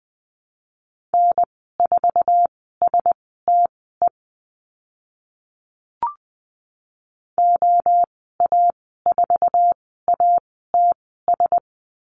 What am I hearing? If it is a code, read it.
Morse code, heard as a keyed tone, about 20 wpm, spaced as for 14 wpm, 700 Hz. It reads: N4STE OA4ATS
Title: 062 Lesson 11g - Letter h - Call Signs 20wpm